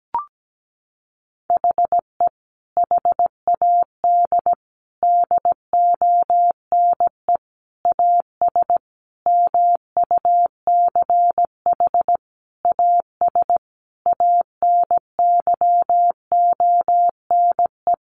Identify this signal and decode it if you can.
Morse code, heard as a keyed tone, about 17 wpm, 700 Hz. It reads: HE HAD DONE AS MUCH AS ANYONE
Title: Sentences from Top 1200 Words 17wpm